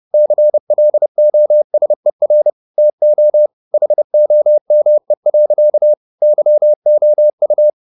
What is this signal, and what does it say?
Morse code, heard as a keyed tone, about 30 wpm, 600 Hz. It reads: CLOSER TO HOME. YOU